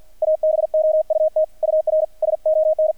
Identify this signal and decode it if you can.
Morse code, heard as a keyed tone, about 48 wpm, 630 Hz, with radio noise. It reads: ABOUT UR SON
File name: archived / CRSnet